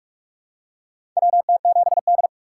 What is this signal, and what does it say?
Morse code, heard as a keyed tone, about 46 wpm, 700 Hz. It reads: WT7D
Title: USA Call Signs - 4 Characters - 3x Character Spacing - 45wpm